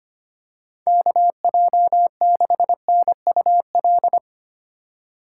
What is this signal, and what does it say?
Morse code, heard as a keyed tone, about 25 wpm, 700 Hz. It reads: KJ6NUL
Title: USA Call Signs - 6 Characters - 4x Character Spacing - 25wpm